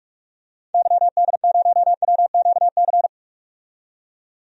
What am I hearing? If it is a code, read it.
Morse code, heard as a keyed tone, about 46 wpm, 700 Hz. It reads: YD0WQC
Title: DXCC-100 Call Signs - 6 Characters - 45wpm